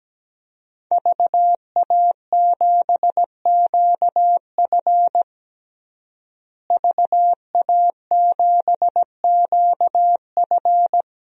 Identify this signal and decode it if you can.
Morse code, heard as a keyed tone, about 17 wpm, 700 Hz. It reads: VA7QF VA7QF
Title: DXCC-100 Call Signs - 5 Characters - Repeated 3x - 17wpm